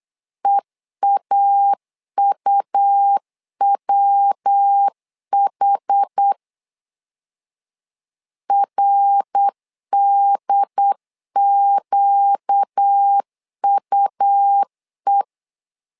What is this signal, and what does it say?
Morse code, heard as a keyed tone, about 8 wpm, 800 Hz. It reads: EAUWH RDQUE
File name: 4 Q 8wpm